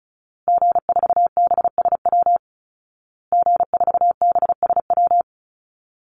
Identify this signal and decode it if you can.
Morse code, heard as a keyed tone, about 34 wpm, 700 Hz. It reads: G4BSW G4BSW
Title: Call Signs Repeated 2x - 35wpm